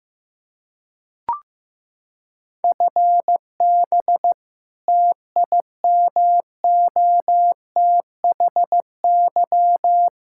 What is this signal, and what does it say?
Morse code, heard as a keyed tone, about 15 wpm, 700 Hz. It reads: FB TIMOTHY
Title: K1USN SST QSO Practice 15wpm